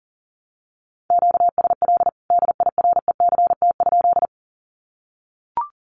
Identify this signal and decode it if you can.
Morse code, heard as a keyed tone, about 40 wpm, 700 Hz. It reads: QSL DIRECT?
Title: Common QSO Phrases 40wpm